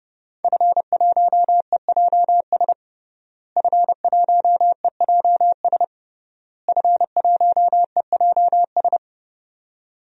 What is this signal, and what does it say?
Morse code, heard as a keyed tone, about 31 wpm, 700 Hz. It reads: F1EJS F1EJS F1EJS
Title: DXCC-100 Call Signs - 5 Characters - Repeated 3x - 30wpm